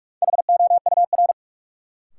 Morse code, heard as a keyed tone, about 46 wpm, 700 Hz. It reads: HOUR
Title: Sentences from Top 400 Words - New Word Drill - 45wpm